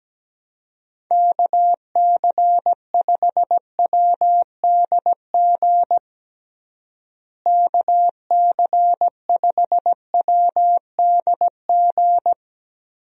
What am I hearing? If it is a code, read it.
Morse code, heard as a keyed tone, about 17 wpm, 700 Hz. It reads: KC5WDG KC5WDG
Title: USA Call Signs - 6 Characters - Repeated 3x - 17wpm